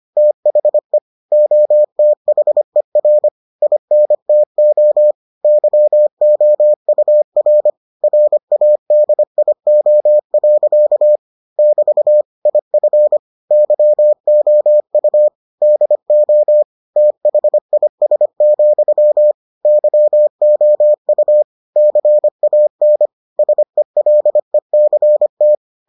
Morse code, heard as a keyed tone, about 25 wpm, 600 Hz. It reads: THE OTHER INTO YOUR RADIO. <BT> IF YOU DO THIS, YOU CAN SELECT